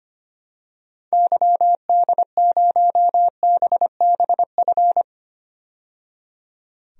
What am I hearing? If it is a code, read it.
Morse code, heard as a keyed tone, about 25 wpm, 700 Hz. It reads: YD0BBF
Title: DXCC-100 Call Signs - 6 Characters - 25wpm